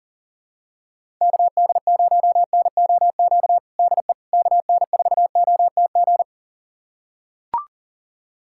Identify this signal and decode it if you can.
Morse code, heard as a keyed tone, about 40 wpm, 700 Hz. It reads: KD0NOQ DE KN4OTG
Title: USA Callsign DE Callsign - 6 Characters - 40wpm